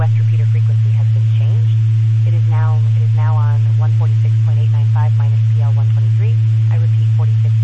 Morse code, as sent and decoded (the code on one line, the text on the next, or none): none